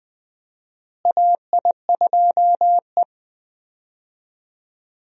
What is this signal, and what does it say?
Morse code, heard as a keyed tone, about 20 wpm, 700 Hz. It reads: AI2E
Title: USA Call Signs - 4 Characters - Repeated 3x - 20wpm